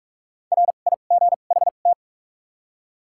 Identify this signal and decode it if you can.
Morse code, heard as a keyed tone, about 46 wpm, spaced as for 32 wpm, 700 Hz. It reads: RIGHT
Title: 2x Character Spacing - Top 200 Words - 45wpm